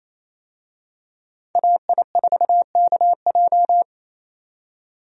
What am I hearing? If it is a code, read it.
Morse code, heard as a keyed tone, about 28 wpm, 700 Hz. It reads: AI4KJ
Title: USA Call Signs - 5 Characters - 28wpm